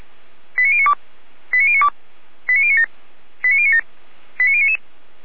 Five-tone selective call (ZVEI), heard as quick runs of five tones, tones of about 70 ms. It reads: 89092 89092 89097 89097 89099